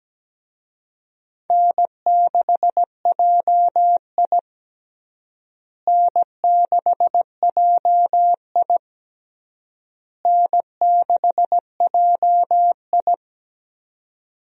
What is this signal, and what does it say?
Morse code, heard as a keyed tone, about 17 wpm, 700 Hz. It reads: N6JI N6JI N6JI
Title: USA Call Signs - 4 Characters - Repeated 3x - 17wpm